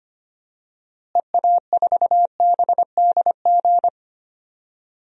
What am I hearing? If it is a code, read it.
Morse code, heard as a keyed tone, about 25 wpm, 700 Hz. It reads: EA4BDG
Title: DXCC-100 Call Signs - 6 Characters - Repeated 3x - 25wpm